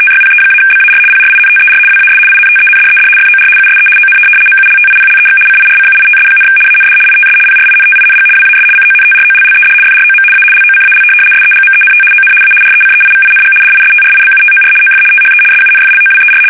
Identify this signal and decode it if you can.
RTTY, about 75 baud. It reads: TYPE-BAR PRINTER WAS INTENDED FOR USE ON MULTIPLEX CIRCUITS, AND ITS PRINTING WAS CONTROLLED FROM A LOCAL SEGMENT ON A RECEIVING DISTRIBUTOR OF THE SUNFLOWER
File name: fsk850-75B / teletype-corp